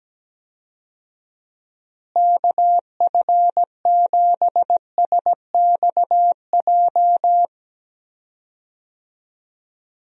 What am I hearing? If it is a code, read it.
Morse code, heard as a keyed tone, about 17 wpm, 700 Hz. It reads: KF7SXJ
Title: USA Call Signs - 6 Characters - Repeated 2x - 17wpm